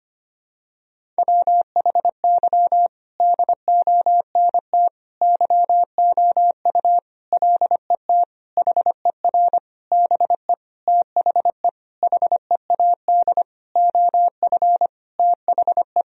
Morse code, heard as a keyed tone, about 25 wpm, 700 Hz. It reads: WHY DONT YOU LET HER BE THE HEAD OF THE
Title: Sentences from Top 1000 Words - Review Part 1 - 25wpm